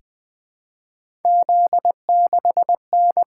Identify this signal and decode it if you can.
Morse code, heard as a keyed tone, about 20 wpm, 700 Hz. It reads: Z6N
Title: Three Letter-Number Combinations - 20wpm